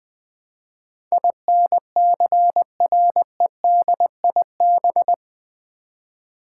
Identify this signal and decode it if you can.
Morse code, heard as a keyed tone, about 20 wpm, 700 Hz. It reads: INCREDIB
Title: Build A Word - 4 Letters At A Time - 10-Letter Words - 20wpm